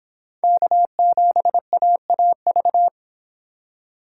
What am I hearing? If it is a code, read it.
Morse code, heard as a keyed tone, about 25 wpm, 700 Hz. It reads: K7AAV